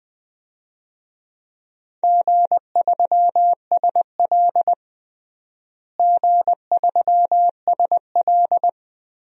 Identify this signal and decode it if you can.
Morse code, heard as a keyed tone, about 21 wpm, 700 Hz. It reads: G3SL G3SL
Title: DXCC-100 Call Signs - 4 Characters - Repeated 2x - 20wpm